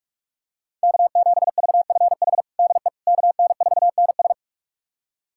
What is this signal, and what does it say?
Morse code, heard as a keyed tone, about 47 wpm, 700 Hz. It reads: K7VFH DE KN4NS
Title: USA Callsign DE Callsign - 5 Characters - 45wpm